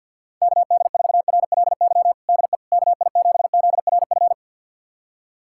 Morse code, heard as a keyed tone, about 48 wpm, 700 Hz. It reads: KD4RLY DE KI7ZRF
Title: USA Callsign DE Callsign 50wpm